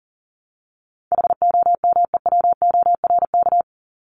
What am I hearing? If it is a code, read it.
Morse code, heard as a keyed tone, about 40 wpm, 700 Hz. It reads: HOMEWORK
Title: Top 1501-1750 Words 40wpm